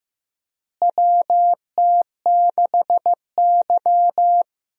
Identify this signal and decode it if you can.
Morse code, heard as a keyed tone, about 15 wpm, 700 Hz. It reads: WT6Y
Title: USA Call Signs - 4 Characters - Repeated 2x - 15wpm